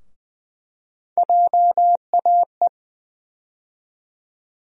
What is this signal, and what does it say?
Morse code, heard as a keyed tone, about 20 wpm, 700 Hz. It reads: JAE
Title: USA Call Sign Postfixes - 3 Letters 20wpm